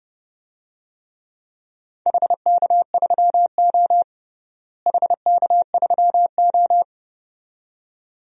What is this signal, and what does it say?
Morse code, heard as a keyed tone, about 32 wpm, 700 Hz. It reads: HK3O HK3O
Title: DXCC-100 Call Signs - 4 Characters - Repeated 2x - 30wpm